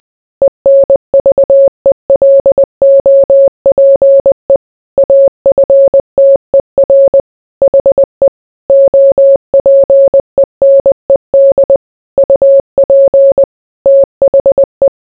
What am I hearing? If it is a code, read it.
Morse code, heard as a keyed tone, about 20 wpm, 550 Hz. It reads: ENVELOPE AFTER HE OPENED UP THE